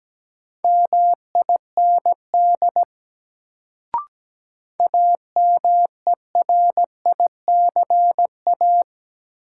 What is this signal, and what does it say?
Morse code, heard as a keyed tone, about 17 wpm, 700 Hz. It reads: MIND AMERICA
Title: Sets of 5 Words - New Word Drill - 17wpm